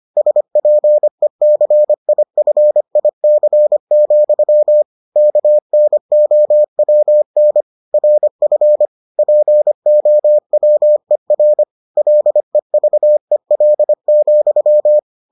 Morse code, heard as a keyed tone, about 25 wpm, 600 Hz. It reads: SPECIFIC, KNOWN RF POWER LEVEL,